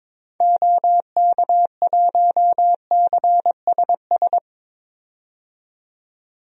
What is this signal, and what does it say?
Morse code, heard as a keyed tone, about 22 wpm, 700 Hz. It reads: OK1CSS
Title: DXCC-100 Call Signs - 6 Characters - 22wpm